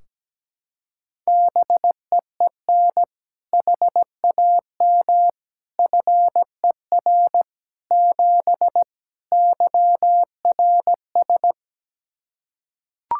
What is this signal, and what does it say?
Morse code, heard as a keyed tone, about 17 wpm, 700 Hz. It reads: BEEN HAM FER 7 YRS
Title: K1USN SST QSO Practice 17wpm